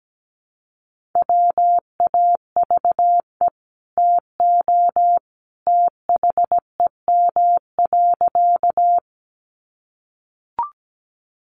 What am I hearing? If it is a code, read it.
Morse code, heard as a keyed tone, about 17 wpm, 700 Hz. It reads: WAVE TO THEM.
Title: Sentences from Top 1400 Words 17wpm